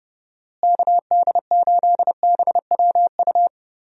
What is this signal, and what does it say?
Morse code, heard as a keyed tone, about 31 wpm, 700 Hz. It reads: KD8BWU
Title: USA Call Signs - 6 Characters - Repeated 2x - 30wpm